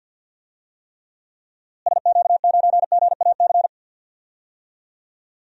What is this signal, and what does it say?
Morse code, heard as a keyed tone, about 52 wpm, 700 Hz. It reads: SQ9GAC
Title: DXCC-100 Call Signs - 6 Characters - Repeated 3x - 50wpm